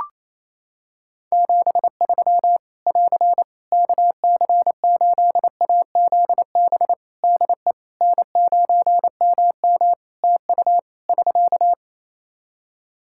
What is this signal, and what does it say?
Morse code, heard as a keyed tone, about 28 wpm, 700 Hz. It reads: 73 <AR> KC8AZB DE N9MM TU <SK>
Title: K1USN SST QSO Practice 28wpm